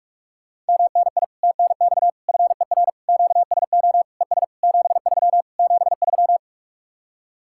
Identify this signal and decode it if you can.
Morse code, heard as a keyed tone, about 59 wpm, spaced as for 52 wpm, 700 Hz. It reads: MNI TNX FER QSO ES 73 73
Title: Speed-Racing Common QSO Phrases Encore - 45wpm